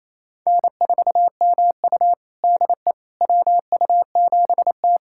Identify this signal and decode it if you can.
Morse code, heard as a keyed tone, about 28 wpm, 700 Hz. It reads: N4MU DE WU7T
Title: USA Callsign DE Callsign - 4 Characters - 28wpm